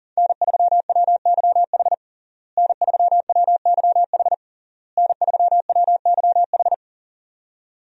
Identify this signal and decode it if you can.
Morse code, heard as a keyed tone, about 40 wpm, 700 Hz. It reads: N3WYH N3WYH N3WYH
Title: USA Call Signs - 5 Characters - Repeated 3x - 40wpm